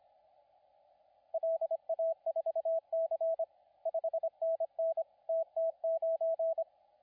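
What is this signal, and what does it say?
Morse code, heard as a keyed tone, about 25 wpm, 660 Hz, with radio noise. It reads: LA4C 5NN TT9
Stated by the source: CW contest audio recorded at LA4C during the SAC CW contest.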